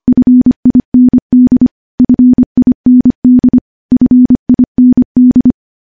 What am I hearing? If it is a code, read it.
Morse code, heard as a keyed tone, about 25 wpm, 250 Hz. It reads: FIND FIND FIND
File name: morse / morse-training-25WPM